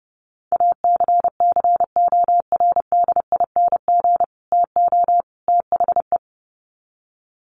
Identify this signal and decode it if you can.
Morse code, heard as a keyed tone, about 30 wpm, 700 Hz. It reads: ACCORDING TO THE
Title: Sets of 3 Words 30wpm